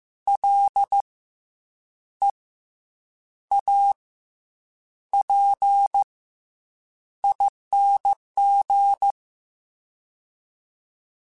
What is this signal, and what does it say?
Morse code, heard as a keyed tone, about 15 wpm, 800 Hz. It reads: L E A P ING
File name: Special INGs / Special INGs-mp3s / SlowToFaster ing @13